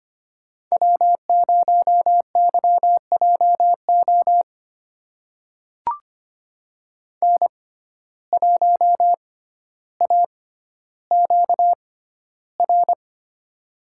Morse code, heard as a keyed tone, about 25 wpm, spaced as for 13 wpm, 700 Hz. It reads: W0YJO N1AQR
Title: USA Call Signs - 5 Characters - 6x Character Spacing - 25wpm